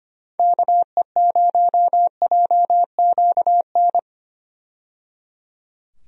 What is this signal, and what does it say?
Morse code, heard as a keyed tone, about 25 wpm, 700 Hz. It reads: KE0JQN
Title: USA Call Signs - 6 Characters - 25wpm